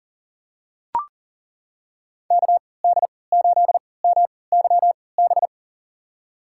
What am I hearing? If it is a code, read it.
Morse code, heard as a keyed tone, about 40 wpm, spaced as for 23 wpm, 700 Hz. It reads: KD8MYB
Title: USA Call Signs - 6 Characters - 3x Character Spacing - 40wpm